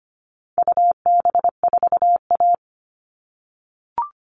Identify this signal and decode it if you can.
Morse code, heard as a keyed tone, about 25 wpm, 700 Hz. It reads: UB4A
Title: All Cut Numbers - 4-Digit 25wpm